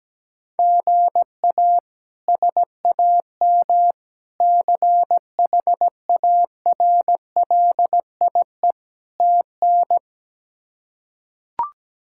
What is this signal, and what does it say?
Morse code, heard as a keyed tone, about 17 wpm, 700 Hz. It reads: GA SAM CHARLIE TN
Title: K1USN SST QSO Practice - 17wpm